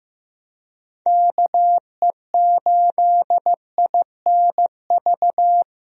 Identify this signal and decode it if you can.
Morse code, heard as a keyed tone, about 16 wpm, 700 Hz. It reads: KE8INV
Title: USA Call Signs - 6 Characters - Repeated 3x - 15wpm